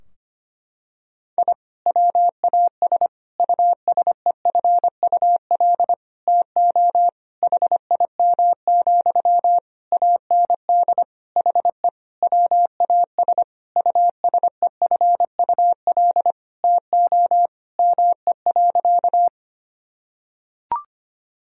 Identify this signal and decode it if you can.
Morse code, heard as a keyed tone, about 25 wpm, 700 Hz. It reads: I WAS USEFUL TO HIM, AND HE WAS USEFUL TO ME.
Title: Sentences from Top 1000 Words - Review Part 4 - 25wpm